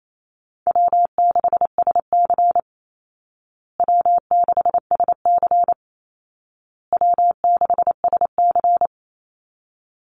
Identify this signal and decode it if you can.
Morse code, heard as a keyed tone, about 28 wpm, 700 Hz. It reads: W6SC W6SC W6SC
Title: Easy Call Signs Repeated 3x - 28wpm